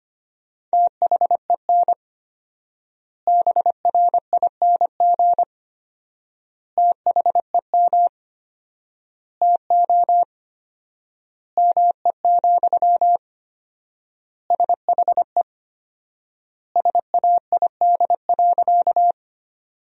Morse code, heard as a keyed tone, about 25 wpm, 700 Hz. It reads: THEN BRING THEM TO ME, SHE SAID.
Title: 4x Word-Spacing - Sentences from Top 500 Words - 25wpm